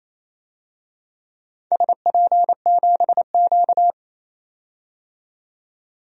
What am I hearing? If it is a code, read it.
Morse code, heard as a keyed tone, about 28 wpm, 700 Hz. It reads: SP7Q
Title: DXCC-100 Call Signs - 4 Characters - 28wpm